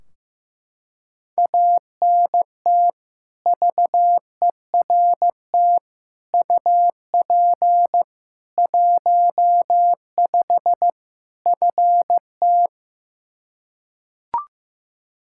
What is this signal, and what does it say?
Morse code, heard as a keyed tone, about 15 wpm, 700 Hz. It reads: ANT VERT UP 15 FT
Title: Practice QSO - Exchange 2 - Info - 15wpm